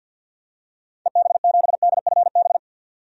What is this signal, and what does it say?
Morse code, heard as a keyed tone, about 52 wpm, 700 Hz. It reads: EB7DFB
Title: DXCC-100 Call Signs - 6 Characters - 50wpm